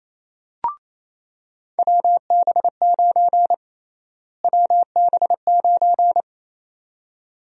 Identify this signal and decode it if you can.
Morse code, heard as a keyed tone, about 27 wpm, 700 Hz. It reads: WB9 WB9
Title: USA Call Sign Prefixes - Repeated 2x - 28wpm